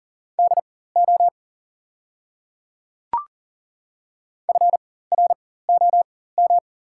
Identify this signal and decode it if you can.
Morse code, heard as a keyed tone, about 40 wpm, spaced as for 19 wpm, 690 Hz. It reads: DO FROM
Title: 4x Character Spacing - Top 300 Words - 40wpm